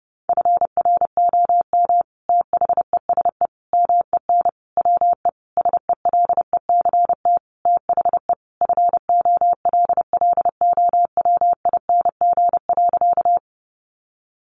Morse code, heard as a keyed tone, about 30 wpm, 700 Hz. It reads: FROM THESE MEN WE SELECT THE FOLLOWING.
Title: Sentences from Top 1300 Words 30wpm